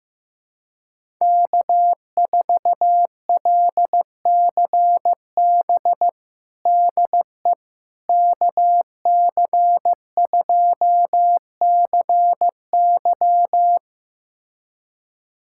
Morse code, heard as a keyed tone, about 15 wpm, 700 Hz. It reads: K4LCB DE KC2CY
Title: USA Callsign DE Callsign - 5 Characters - 15wpm